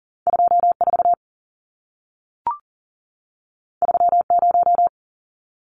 Morse code, heard as a keyed tone, about 40 wpm, 700 Hz. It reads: 24 30
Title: Numbers - 2-Digit 40wpm